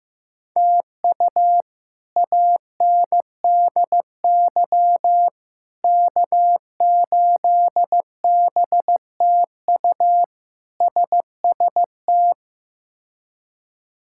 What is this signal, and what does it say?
Morse code, heard as a keyed tone, about 15 wpm, 700 Hz. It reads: TU ANDY K8BTU SST
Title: K1USN SST QSO Practice - 15wpm